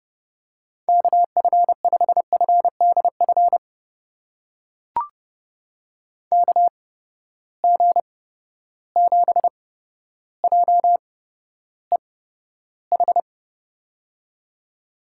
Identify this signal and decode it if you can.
Morse code, heard as a keyed tone, about 32 wpm, spaced as for 13 wpm, 700 Hz. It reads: KF5FDF KG7JEH
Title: USA Call Signs - 8x Character Spacing - 30wpm